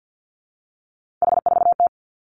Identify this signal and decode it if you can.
Morse code, heard as a keyed tone, about 50 wpm, 700 Hz. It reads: H4T
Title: Three Challenging Letter-Number Combinations 50wpm